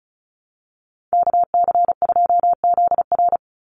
Morse code, heard as a keyed tone, about 34 wpm, 700 Hz. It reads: KC2ZR
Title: USA Call Signs 35wpm